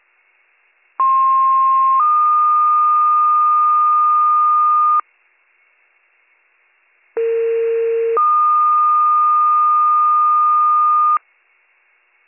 Two-tone page, heard as a one-second tone, then a three-second tone, about 1060 then 1220 Hz, about 460 then 1220 Hz. It is heard from 1.0 to 11.2 s.